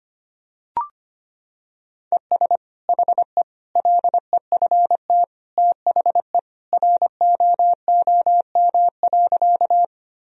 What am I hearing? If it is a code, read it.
Morse code, heard as a keyed tone, about 25 wpm, 700 Hz. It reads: ES HE LEFT THE ROOM.
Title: Sentences from Top 300 Words and Abbreviations 25wpm